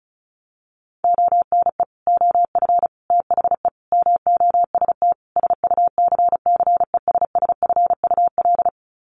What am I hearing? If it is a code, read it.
Morse code, heard as a keyed tone, about 34 wpm, 700 Hz. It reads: ONE OF THE MOST SUCCESSFUL
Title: Sets of 5 Words Encore2 35wpm